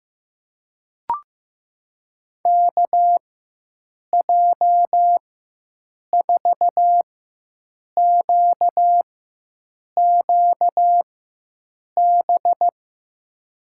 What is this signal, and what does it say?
Morse code, heard as a keyed tone, about 15 wpm, spaced as for 7 wpm, 700 Hz. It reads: KJ4QQB
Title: USA Call Signs - 6 Characters - 4x Character Spacing - 15wpm